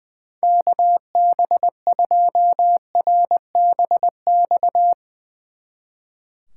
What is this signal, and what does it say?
Morse code, heard as a keyed tone, about 20 wpm, 700 Hz. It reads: KB2RBX
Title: Speed Racing - USA 2x3 Call Signs 20wpm